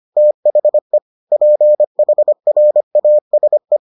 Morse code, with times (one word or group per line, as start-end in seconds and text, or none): 0.2-1.0 THE
1.3-3.8 PHRASE